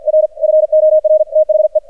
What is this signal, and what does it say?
Morse code, heard as a keyed tone, about 51 wpm, 620 Hz, with radio noise. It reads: R FOR THE
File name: archived / CRSnet